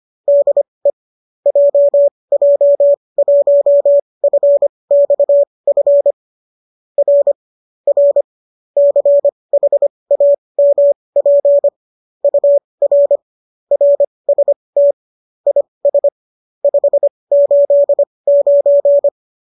Morse code, heard as a keyed tone, about 25 wpm, spaced as for 20 wpm, 570 Hz. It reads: DE JJ1FXF R R CHAMP UR RST IS 589
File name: qso208 20